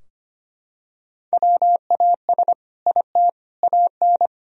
Morse code, heard as a keyed tone, about 25 wpm, 700 Hz. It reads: WAS IT AN
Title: Sentences from Top 1000 Words - Review Part 4 - 25wpm